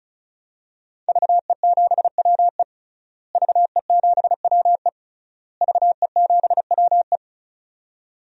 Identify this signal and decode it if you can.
Morse code, heard as a keyed tone, about 36 wpm, 700 Hz. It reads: VE7WE VE7WE VE7WE
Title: DXCC-100 Call Signs - 5 Characters - Repeated 3x - 35wpm